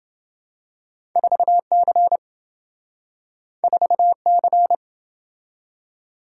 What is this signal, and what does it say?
Morse code, heard as a keyed tone, about 27 wpm, 700 Hz. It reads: 4C 4C